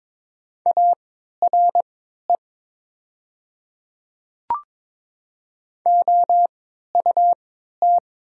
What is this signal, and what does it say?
Morse code, heard as a keyed tone, about 22 wpm, spaced as for 12 wpm, 700 Hz. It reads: ARE OUT
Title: 3x Character Spacing - Top 100 Words - 22wpm